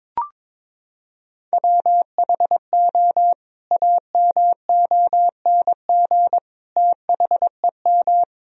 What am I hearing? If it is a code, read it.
Morse code, heard as a keyed tone, about 22 wpm, 700 Hz. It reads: WHO AMONG THEM
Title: Sentences from Top 1000 Words - Review Part 3 - 22wpm